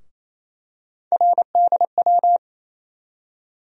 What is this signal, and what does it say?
Morse code, heard as a keyed tone, about 28 wpm, 700 Hz. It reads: RDW
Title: USA Call Sign Postfixes - 3 Letters 28wpm